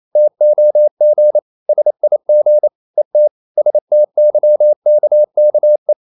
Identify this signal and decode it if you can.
Morse code, heard as a keyed tone, about 28 wpm, 600 Hz. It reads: TOG SIG ET STYKKE
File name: Ferie QRQ 28wpm